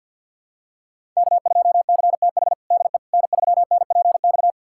Morse code, heard as a keyed tone, about 51 wpm, 700 Hz. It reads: K2CTH DE N3NPX
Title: USA Callsign DE Callsign - 5 Characters - 50wpm